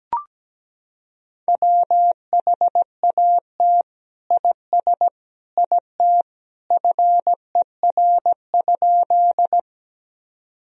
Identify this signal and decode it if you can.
Morse code, heard as a keyed tone, about 17 wpm, 700 Hz. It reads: WHAT IS IT FER?
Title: Sentences from Top 100 Words and Abbreviations 17wpm